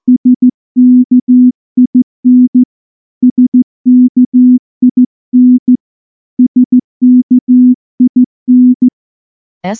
Morse code, heard as a keyed tone, about 15 wpm, 250 Hz. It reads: SKIN SKIN SKIN
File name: morse-training-15WPM